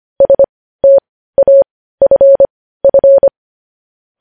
Morse code, heard as a keyed tone, about 25 wpm, spaced as for 15 wpm, 550 Hz. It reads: STAFF
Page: news headlines in morse code at 15 wpm